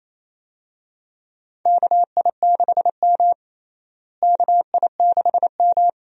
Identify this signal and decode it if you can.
Morse code, heard as a keyed tone, about 28 wpm, 700 Hz. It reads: KI6M KI6M
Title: USA Call Signs - 4 Characters - Repeated 2x - 28wpm